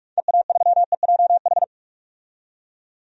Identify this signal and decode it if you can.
Morse code, heard as a keyed tone, about 45 wpm, 700 Hz. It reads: EA3EJH
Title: DXCC-100 Call Signs - 6 Characters - 45wpm